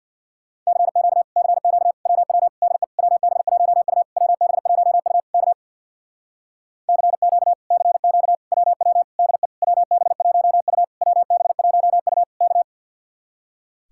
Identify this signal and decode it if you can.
Morse code, heard as a keyed tone, about 64 wpm, 700 Hz. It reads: CQ CQ WW DE WB1U WB1U K CQ CQ WW DE WB1U WB1U K